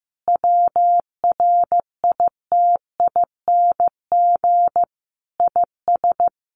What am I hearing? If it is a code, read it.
Morse code, heard as a keyed tone, about 15 wpm, 700 Hz. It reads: WRITING IS
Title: Sentences from Top 800 Words 15wpm